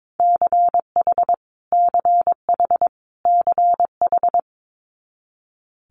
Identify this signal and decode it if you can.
Morse code, heard as a keyed tone, about 22 wpm, 700 Hz. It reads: CH CH CH
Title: Multi-letter Phonemes Repeated 3x 22wpm